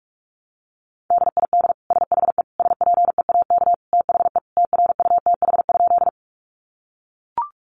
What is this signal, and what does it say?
Morse code, heard as a keyed tone, about 44 wpm, 700 Hz. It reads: DID SHE SPEAK THE TRUTH?
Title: Sentences from Top 1100 Words 45wpm